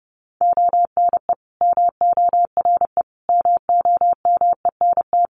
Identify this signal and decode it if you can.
Morse code, heard as a keyed tone, about 30 wpm, 700 Hz. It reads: ONE MORE MOMENT
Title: Sentences from Top 1300 Words 30wpm